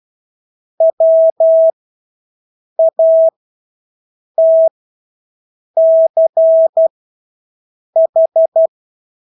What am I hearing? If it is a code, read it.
Morse code, heard as a keyed tone, about 12 wpm, spaced as for 6 wpm, 650 Hz. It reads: WATCH
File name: Engelske ord 12-6 wpm